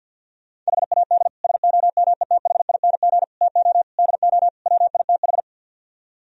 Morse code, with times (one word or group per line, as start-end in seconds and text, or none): 0.7-1.3 HAD
1.4-3.2 SOMETHING
3.4-3.8 TO
4.0-4.5 DO
4.7-5.4 WITH